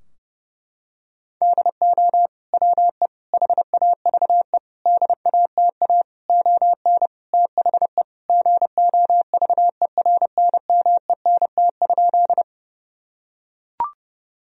Morse code, heard as a keyed tone, about 30 wpm, 700 Hz. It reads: DO WE HAVE DATA ON THE GOVERNMENT?
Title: Sentences from Top 1000 Words - Review Part 3 - 30wpm